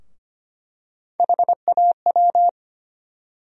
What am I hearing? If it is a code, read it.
Morse code, heard as a keyed tone, about 25 wpm, 700 Hz. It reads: HAW
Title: USA Call Sign Postfixes - 3 Letters 25wpm